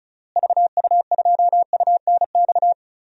Morse code, heard as a keyed tone, about 36 wpm, 700 Hz. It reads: VU2UNX
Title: DXCC-100 Call Signs - 6 Characters - Repeated 3x - 35wpm